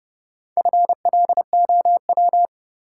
Morse code, heard as a keyed tone, about 30 wpm, 700 Hz. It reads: FLOW
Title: Sentences from Top 1200 Words - New Word Drill - 30wpm